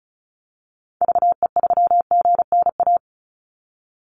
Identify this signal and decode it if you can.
Morse code, heard as a keyed tone, about 34 wpm, 700 Hz. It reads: VE3GNA
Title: Call Signs - 35wpm